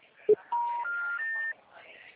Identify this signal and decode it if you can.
Special information tone, with speech.